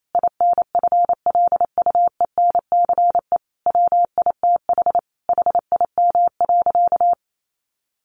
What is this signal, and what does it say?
Morse code, heard as a keyed tone, about 28 wpm, 700 Hz. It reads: INFLUENCE WITH HIM.
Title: Sentences from Top 1100 Words 28wpm